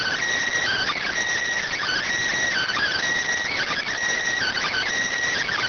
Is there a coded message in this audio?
Yes: Emergency Alert System (SAME) tones (header bursts) from 0.6 to 4.9 s, heard as harsh buzzing bursts.